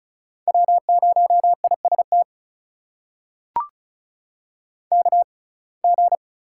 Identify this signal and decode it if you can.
Morse code, heard as a keyed tone, about 36 wpm, spaced as for 25 wpm, 700 Hz. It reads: W0IST KG
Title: USA Call Signs - 5 Characters - 6x Character Spacing - 35wpm